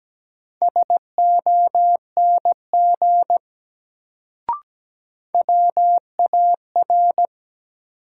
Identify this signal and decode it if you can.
Morse code, heard as a keyed tone, about 17 wpm, 700 Hz. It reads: SONG WAR
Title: Sentences from Top 400 Words - New Word Drill - 17wpm